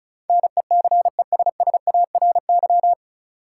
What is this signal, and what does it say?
Morse code, heard as a keyed tone, about 34 wpm, 700 Hz. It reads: NECESSARY
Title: Sentences from Top 800 Words - New Word Drill - 35wpm